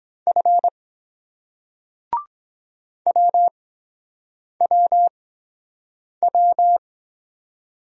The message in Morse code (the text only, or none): F WWW